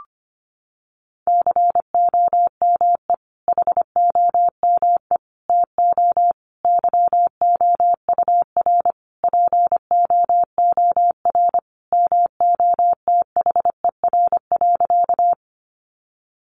Morse code, heard as a keyed tone, about 25 wpm, 700 Hz. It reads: COME HOME TO YOUR POOR MOTHER.
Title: Sentences from Top 1000 Words 25wpm